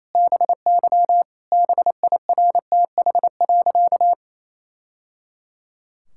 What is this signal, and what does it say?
Morse code, heard as a keyed tone, about 28 wpm, 700 Hz. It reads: BY BIRTH.
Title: Sentences from Top 1000 Words - Review Part 5 - 28wpm